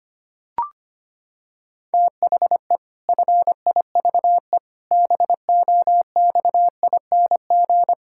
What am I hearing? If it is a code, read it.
Morse code, heard as a keyed tone, about 25 wpm, 700 Hz. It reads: THE FIVE BOXING
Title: Pangram Challenge v2 - 25wpm